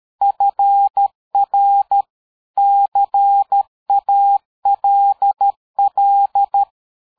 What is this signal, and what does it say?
Morse code, heard as a keyed tone, about 13 wpm, 800 Hz. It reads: FR CALL
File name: dk5ke / QSO 60BpM